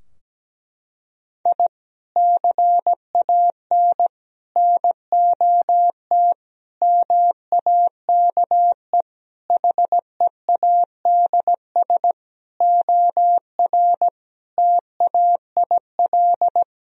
Morse code, heard as a keyed tone, about 17 wpm, 700 Hz. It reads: I CAN NOT MAKE HEADS OR TAIL
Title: Sentences from Top 1000 Words - Review Part 3 - 17wpm